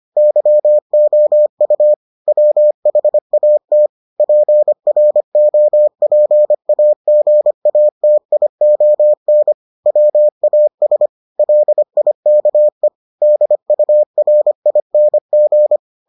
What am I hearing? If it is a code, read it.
Morse code, heard as a keyed tone, about 25 wpm, 600 Hz. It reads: YOU WHAT PROPAGATION WAS LIKE DURING